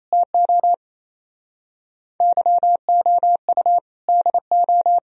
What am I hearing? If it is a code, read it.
Morse code, heard as a keyed tone, about 36 wpm, 700 Hz. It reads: TO YOU DO